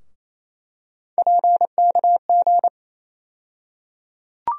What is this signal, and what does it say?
Morse code, heard as a keyed tone, about 28 wpm, 700 Hz. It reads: PKG
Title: USA Call Sign Postfixes - 3 Letters 28wpm